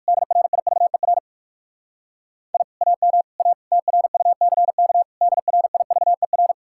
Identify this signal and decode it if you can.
Morse code, heard as a keyed tone, about 58 wpm, 700 Hz. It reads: DRIVER I AM A TRUCK DRIVER